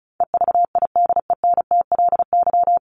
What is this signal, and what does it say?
Morse code, heard as a keyed tone, about 34 wpm, 700 Hz. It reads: EVIDENTLY
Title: LY Suffix - Top 100 Words with Base Words 35wpm